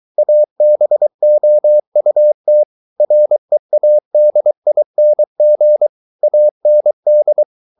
Morse code, heard as a keyed tone, about 23 wpm, 600 Hz. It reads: ABOUT READING AND